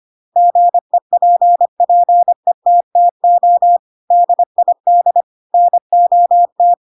Morse code, heard as a keyed tone, about 25 wpm, 700 Hz. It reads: GEPPETTO DID NOT
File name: Pinocchio / Pinocchio ch8 25 25 ik1hge